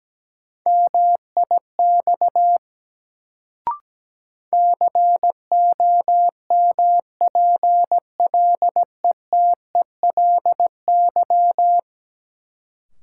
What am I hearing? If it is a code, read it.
Morse code, heard as a keyed tone, about 17 wpm, 700 Hz. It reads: MIX COMPLETELY
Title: Sentences from Top 1100 Words - New Word Drill Alt - 17wpm